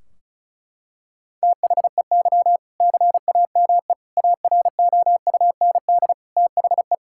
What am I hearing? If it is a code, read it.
Morse code, heard as a keyed tone, about 36 wpm, 700 Hz. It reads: THEY CAME AROUND THE
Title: Sentences from Top 1600 Words 35wpm